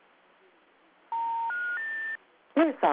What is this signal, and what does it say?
Special information tone.